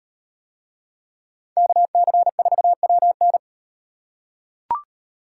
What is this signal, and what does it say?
Morse code, heard as a keyed tone, about 38 wpm, 700 Hz. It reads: KC4WN